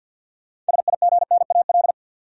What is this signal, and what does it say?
Morse code, heard as a keyed tone, about 52 wpm, 700 Hz. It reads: SIGNAL
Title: Sentences from Top 1400 Words - New Word Drill - 50wpm